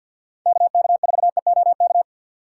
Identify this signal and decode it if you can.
Morse code, heard as a keyed tone, about 47 wpm, 700 Hz. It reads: KK4EOK
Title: USA Call Signs - 6 Characters - 4x Character Spacing - 50wpm